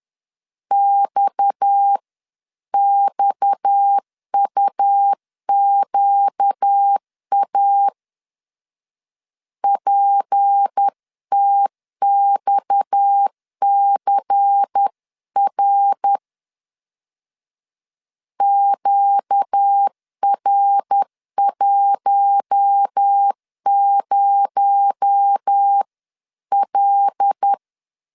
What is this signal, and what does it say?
Morse code, heard as a keyed tone, about 10 wpm, 800 Hz. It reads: XXUQA PTXCR QR10L
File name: CW / DET / 1 6 1500C S B 10wpm 4137